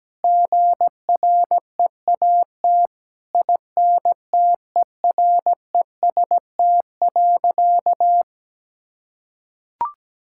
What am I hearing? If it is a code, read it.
Morse code, heard as a keyed tone, about 17 wpm, 700 Hz. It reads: GREAT INTEREST.
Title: Sentences from Top 1000 Words - Review Part 6 - 17wpm